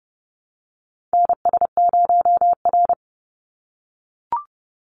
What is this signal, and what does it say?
Morse code, heard as a keyed tone, about 30 wpm, 700 Hz. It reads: NS0R
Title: Call Signs - 30wpm